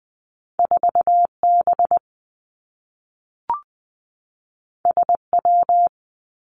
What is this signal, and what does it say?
Morse code, heard as a keyed tone, about 19 wpm, 700 Hz. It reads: 4 B S W
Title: Two Challenging Letter-Number Combinations 20wpm